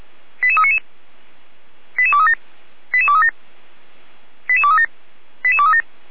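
Five-tone selective call (ZVEI), heard as quick runs of five tones, tones of about 70 ms. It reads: 88399 88237 88237 88237 88237
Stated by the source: ZVEI five-tone call 88307 received via SDR.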